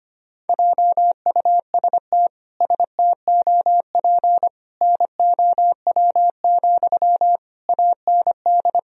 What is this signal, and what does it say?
Morse code, heard as a keyed tone, about 25 wpm, 700 Hz. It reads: JUST STOP NOW, AND
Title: Sentences from Top 1000 Words - Review Part 1 - 25wpm